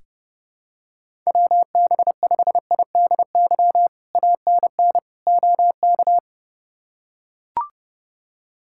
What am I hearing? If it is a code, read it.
Morse code, heard as a keyed tone, about 28 wpm, 700 Hz. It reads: WB5IDY ANN OK
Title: K1USN SST QSO Practice - 30wpm